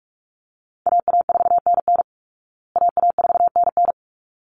Morse code, heard as a keyed tone, about 44 wpm, 700 Hz. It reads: AA4NN AA4NN
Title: Call Signs Repeated 3x - 45wpm